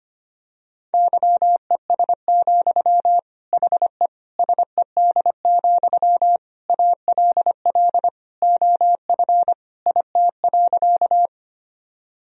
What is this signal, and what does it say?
Morse code, heard as a keyed tone, about 25 wpm, 700 Hz. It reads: YES, HE SED, ALL OF IT.
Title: Sentences from Top 500 Words and Abbreviations 25wpm